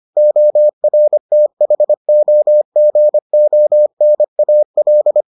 Morse code, heard as a keyed tone, about 25 wpm, 600 Hz. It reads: ORTHOGONAL